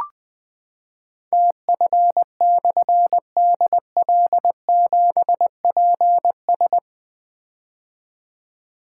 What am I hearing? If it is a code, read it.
Morse code, heard as a keyed tone, about 20 wpm, 700 Hz. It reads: TF/DL7PS
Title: Difficult Call Signs 20wpm